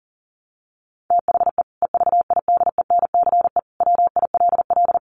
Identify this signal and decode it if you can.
Morse code, heard as a keyed tone, about 40 wpm, 700 Hz. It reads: THE EVIDENCE WILL
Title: Sentences from Top 1300 Words 40wpm